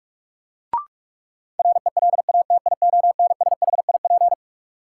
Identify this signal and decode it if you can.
Morse code, heard as a keyed tone, about 46 wpm, 700 Hz. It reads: RELATIONSHIP
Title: Sentences from Top 1100 Words - New Word Drill - 45wpm